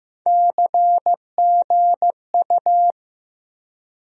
Morse code, heard as a keyed tone, about 15 wpm, 700 Hz. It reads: CGU
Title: USA Call Sign Postfixes - 3 Letters 15wpm